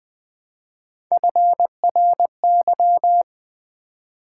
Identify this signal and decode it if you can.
Morse code, heard as a keyed tone, about 20 wpm, 700 Hz. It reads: FRY
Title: Build A Word - 1 Letter At A Time - 3-Letter Words - 20wpm